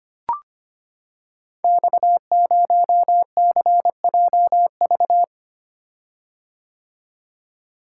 Morse code, heard as a keyed tone, about 25 wpm, 690 Hz. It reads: X0CJV
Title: Code Groups - 5 Letter:Numbers - 25wpm